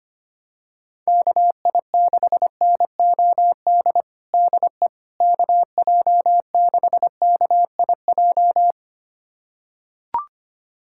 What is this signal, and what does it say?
Morse code, heard as a keyed tone, about 25 wpm, 700 Hz. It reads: KI6NOD DE KJ6KIJ
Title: USA Callsign DE Callsign - 6 Characters - 25wpm